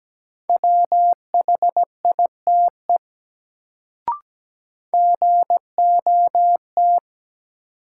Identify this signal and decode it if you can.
Morse code, heard as a keyed tone, about 17 wpm, 700 Hz. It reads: WHITE GOT
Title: Sentences from Top 300 Words - New Word Drill - 17wpm